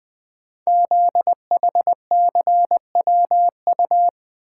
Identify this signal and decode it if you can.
Morse code, heard as a keyed tone, about 20 wpm, 700 Hz. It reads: ZHCWU
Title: Code Groups - 5 Letters - 20wpm